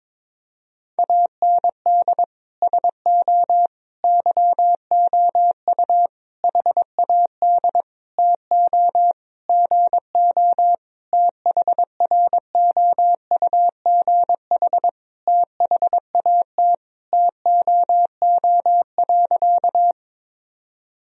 Morse code, heard as a keyed tone, about 22 wpm, 700 Hz. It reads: AND SO YOU HAD TO GO THROUGH THAT TOO.
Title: Sentences from Top 1000 Words - Review Part 1 - 22wpm